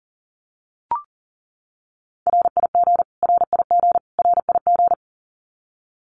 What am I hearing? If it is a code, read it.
Morse code, heard as a keyed tone, about 40 wpm, 700 Hz. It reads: RIG RIG RIG
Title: Abbreviations Repeated 3x - 40wpm